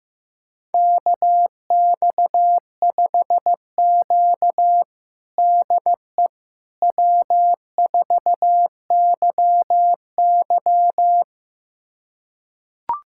Morse code, heard as a keyed tone, about 16 wpm, 700 Hz. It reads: KX5Q DE W4YY
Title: USA Callsign DE Callsign - 4 Characters - 15wpm